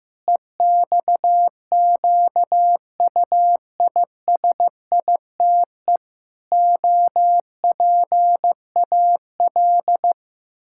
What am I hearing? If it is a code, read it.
Morse code, heard as a keyed tone, about 16 wpm, 700 Hz. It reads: EXQUISITE OPAL
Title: Pangram Challenge v2 - 15wpm